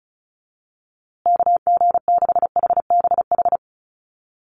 Morse code, heard as a keyed tone, about 34 wpm, 700 Hz. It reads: KG6HBH
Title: USA Call Signs Repeated 2x 35wpm